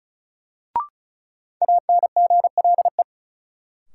Morse code, heard as a keyed tone, about 34 wpm, 700 Hz. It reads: ANGLE
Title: Sentences from Top 1500 Words - New Word Drill Alt - 35wpm